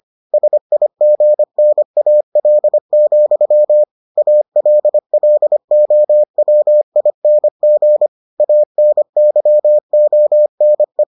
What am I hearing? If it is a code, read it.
Morse code, heard as a keyed tone, about 25 wpm, 600 Hz. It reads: SIGNAL, ALLOWING ANYONE